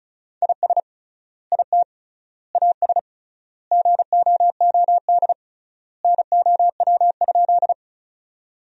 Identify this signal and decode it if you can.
Morse code, heard as a keyed tone, about 36 wpm, 700 Hz. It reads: IS IT AS GOOD NOW?
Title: 3x Word-Spacing - Sentences from Top 200 Words - 35wpm